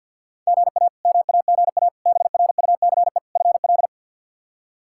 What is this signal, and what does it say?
Morse code, heard as a keyed tone, about 46 wpm, 700 Hz. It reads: GA MAGA BRUCE FL
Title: K1USN SST QSO Practice - 50wpm